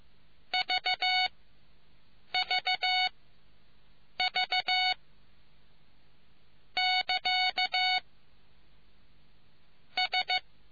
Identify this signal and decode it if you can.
Morse code, heard as a keyed tone, about 15 wpm, spaced as for 6 wpm, 740 Hz. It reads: VVV <KA> S